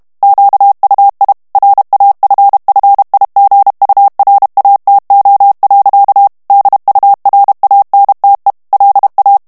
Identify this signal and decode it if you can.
Morse code, heard as a keyed tone, about 32 wpm, 800 Hz. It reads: QUI RAFFIGURATO. DURANTE LA
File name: morse mp3 / ITALIEN texte clair 1 vitesse 30 mots minute